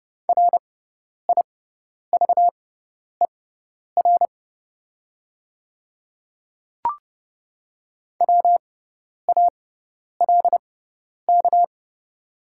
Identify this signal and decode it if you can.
Morse code, heard as a keyed tone, about 30 wpm, spaced as for 10 wpm, 700 Hz. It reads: RIVER WALK
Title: 6x Character Spacing - Top 300 Words - 30wpm